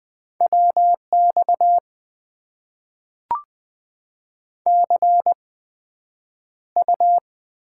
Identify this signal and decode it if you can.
Morse code, heard as a keyed tone, about 20 wpm, spaced as for 9 wpm, 700 Hz. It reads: WX CU
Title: Top 90 QSO Elements - 8x Character Spacing - 20wpm